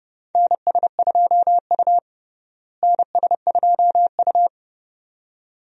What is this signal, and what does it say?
Morse code, heard as a keyed tone, about 32 wpm, 700 Hz. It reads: NS2U NS2U
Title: USA Call Signs - 4 Characters - Repeated 2x - 30wpm